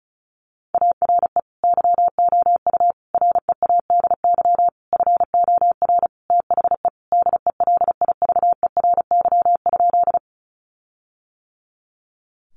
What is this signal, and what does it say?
Morse code, heard as a keyed tone, about 34 wpm, 700 Hz. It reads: ARE YOU READY FOR THE DELIVERY?
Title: Sentences from Top 1500 Words 35wpm